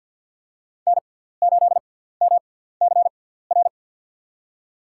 Morse code, heard as a keyed tone, about 53 wpm, spaced as for 17 wpm, 700 Hz. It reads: N8MCR
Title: USA Call Signs - 5 Characters - 6x Character Spacing - 50wpm